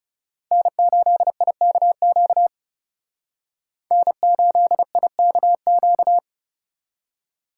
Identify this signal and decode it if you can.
Morse code, heard as a keyed tone, about 39 wpm, 700 Hz. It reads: N8IKQ N8IKQ